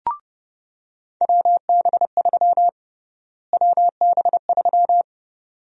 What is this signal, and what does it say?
Morse code, heard as a keyed tone, about 30 wpm, 700 Hz. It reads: WB3 WB3
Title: USA Call Sign Prefixes - Repeated 2x - 30wpm